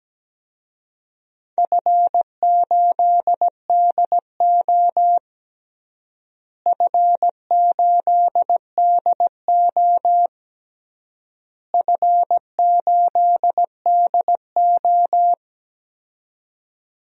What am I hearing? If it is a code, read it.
Morse code, heard as a keyed tone, about 17 wpm, 700 Hz. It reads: F8DO F8DO F8DO
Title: DXCC-100 Call Signs - 4 Characters - Repeated 3x - 17wpm